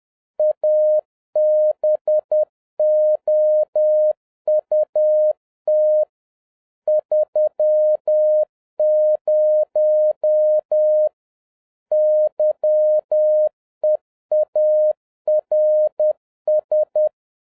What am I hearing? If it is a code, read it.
Morse code, heard as a keyed tone, about 10 wpm, 610 Hz. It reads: ABOUT 30 YEARS